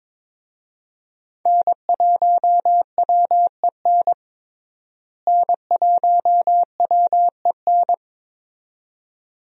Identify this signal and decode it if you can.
Morse code, heard as a keyed tone, about 21 wpm, 700 Hz. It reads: N1WEN N1WEN
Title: USA Call Signs - 5 Characters - Repeated 2x - 22wpm